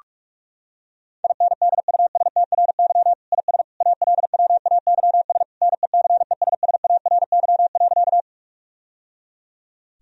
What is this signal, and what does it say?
Morse code, heard as a keyed tone, about 46 wpm, 700 Hz. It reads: INDUSTRY IS ALWAYS NECESSARY.
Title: Sentences from Top 1000 Words - Review Part 4 - 45wpm